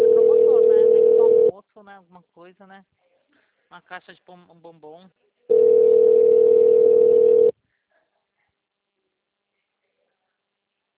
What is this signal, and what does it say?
Ringback tone (North America), 440 + 480 Hz, with speech.
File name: out- -2001- - - .55367